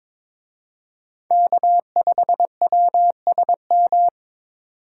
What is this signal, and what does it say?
Morse code, heard as a keyed tone, about 26 wpm, 700 Hz. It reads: K5WSM
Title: Speed Racing - USA 1x3 Call Signs 22wpm